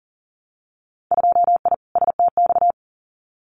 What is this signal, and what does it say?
Morse code, heard as a keyed tone, about 40 wpm, 700 Hz. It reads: 2I STX
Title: Winter Field Day Exchanges 40wpm